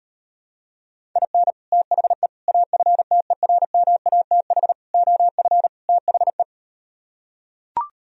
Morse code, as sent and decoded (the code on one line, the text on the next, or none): .. -. / - .... / .- ..-. - . .-. -- .- - .... / --- ..-. / - .... .
IN THE AFTERMATH OF THE